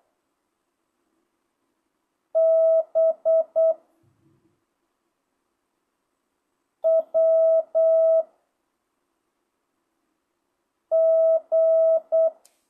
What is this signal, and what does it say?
Morse code, heard as a keyed tone, about 8 wpm, 1300 Hz. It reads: B W G
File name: jh8cbh / 2025cw